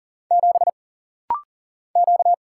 Morse code, heard as a keyed tone, about 40 wpm, 700 Hz. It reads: 7 Q
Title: Single Letter-Number Alt - Rapid-Fire 40wpm